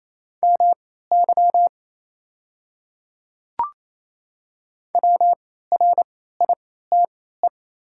Morse code, heard as a keyed tone, about 27 wpm, spaced as for 16 wpm, 700 Hz. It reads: MY WRITE
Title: 3x Character Spacing - Top 100 Words - 28wpm